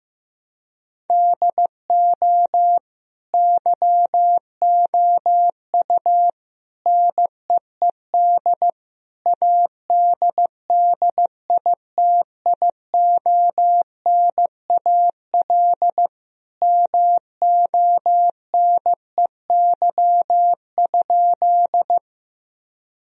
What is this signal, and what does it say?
Morse code, heard as a keyed tone, about 15 wpm, 700 Hz. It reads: DO YOU NEED ADDITIONAL MONEY?
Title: Sentences from Top 1000 Words - Review Part 4 - 15wpm